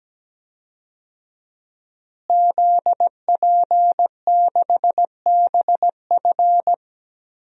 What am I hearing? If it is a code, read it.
Morse code, heard as a keyed tone, about 17 wpm, 700 Hz. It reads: ZP6BF
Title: DXCC-100 Call Signs - 5 Characters - Repeated 3x - 17wpm